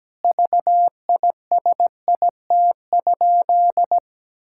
Morse code, heard as a keyed tone, about 17 wpm, 700 Hz. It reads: VISIT?
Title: Sentences from Top 1600 Words 17wpm